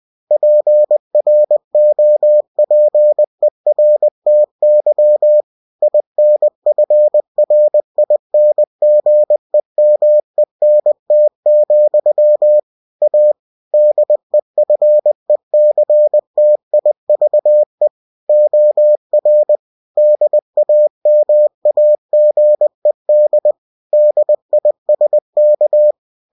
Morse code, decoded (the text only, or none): PROPERTY INFRINGEMENT, A DEFECTIVE OR DAMAGED DISK